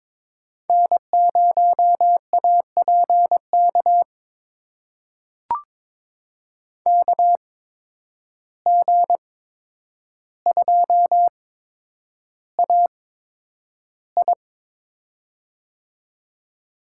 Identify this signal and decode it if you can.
Morse code, heard as a keyed tone, about 22 wpm, spaced as for 9 wpm, 700 Hz. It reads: N0APK KG2AI
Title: USA Call Signs - 5 Characters - 8x Character Spacing - 22wpm